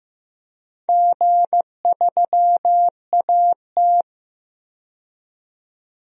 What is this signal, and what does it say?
Morse code, heard as a keyed tone, about 15 wpm, 700 Hz. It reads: G3AT
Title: DXCC-100 Call Signs - 4 Characters - 15wpm